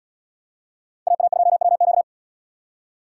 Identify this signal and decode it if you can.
Morse code, heard as a keyed tone, about 77 wpm, 700 Hz. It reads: AA2WQ